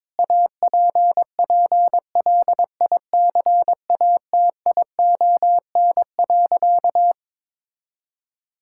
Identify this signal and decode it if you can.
Morse code, heard as a keyed tone, about 22 wpm, 700 Hz. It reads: APPLICATION.
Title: Sentences from Top 1600 Words 22wpm